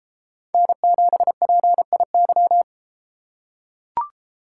Morse code, heard as a keyed tone, about 32 wpm, 700 Hz. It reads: N7PIY